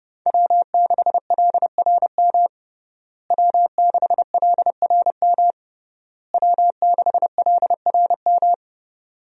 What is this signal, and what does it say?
Morse code, heard as a keyed tone, about 30 wpm, 700 Hz. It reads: W6LRM W6LRM W6LRM
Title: USA Call Signs - 5 Characters - Repeated 3x - 30wpm